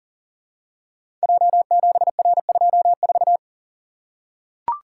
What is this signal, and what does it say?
Morse code, heard as a keyed tone, about 38 wpm, 700 Hz. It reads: J7R24
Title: Code Groups - 5 Letter:Numbers - 40wpm